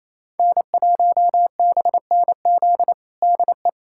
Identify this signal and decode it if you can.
Morse code, heard as a keyed tone, about 28 wpm, 700 Hz. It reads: N1BNZ DE
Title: USA Callsign DE Callsign - 5 Characters - 28wpm